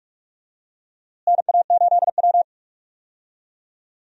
Morse code, heard as a keyed tone, about 46 wpm, 700 Hz. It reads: NA8W
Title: USA Call Signs - 4 Characters - Repeated 3x - 45wpm